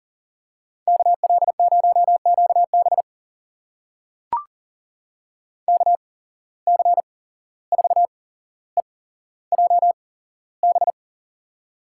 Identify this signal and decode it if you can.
Morse code, heard as a keyed tone, about 40 wpm, spaced as for 16 wpm, 700 Hz. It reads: KL0QB KC4EJB
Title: USA Call Signs - 8x Character Spacing - 40wpm